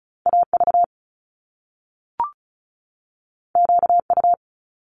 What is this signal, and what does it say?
Morse code, heard as a keyed tone, about 36 wpm, 700 Hz. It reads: AV QU
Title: Top Two Letter Combinations in Words 35wpm